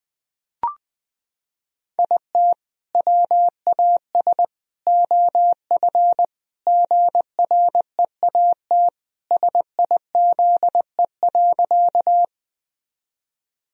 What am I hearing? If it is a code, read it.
Morse code, heard as a keyed tone, about 21 wpm, 700 Hz. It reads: IT WAS OF GREAT SIZE.
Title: Sentences from Top 1000 Words - Review Part 2 - 20wpm